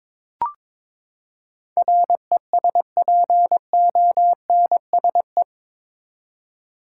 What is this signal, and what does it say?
Morse code, heard as a keyed tone, about 21 wpm, 700 Hz. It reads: RESPONSE
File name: practice / 22wpm / Eight Letter Words - 22wpm